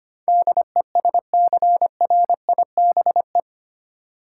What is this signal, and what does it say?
Morse code, heard as a keyed tone, about 25 wpm, 700 Hz. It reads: DESCRIBE
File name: Eight Letter Words - 25wpm